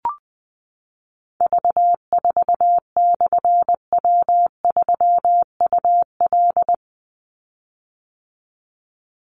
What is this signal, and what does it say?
Morse code, heard as a keyed tone, about 20 wpm, 700 Hz. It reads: V4/W3UL
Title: Call Signs - 20wpm